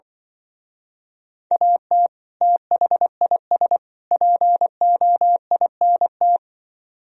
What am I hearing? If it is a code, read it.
Morse code, heard as a keyed tone, about 25 wpm, 700 Hz. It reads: AT THIS POINT
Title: Speed-Racing - Sets of 3 Words 20wpm